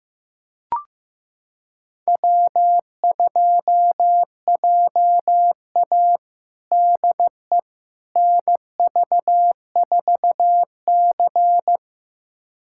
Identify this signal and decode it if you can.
Morse code, heard as a keyed tone, about 15 wpm, 700 Hz. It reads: W2JA DE NV4C
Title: USA Callsign DE Callsign - 4 Characters - 15wpm